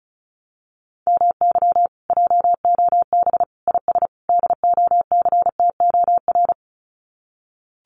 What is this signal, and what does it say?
Morse code, heard as a keyed tone, about 34 wpm, 700 Hz. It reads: MY JOB IS DOCTOR
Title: Common QSO Phrases Encore 35wpm